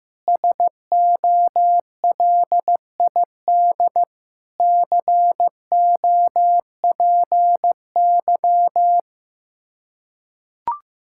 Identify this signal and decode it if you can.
Morse code, heard as a keyed tone, about 15 wpm, 700 Hz. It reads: SOLID COPY
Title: Practice QSO - Exchange 3 - More Info - 15wpm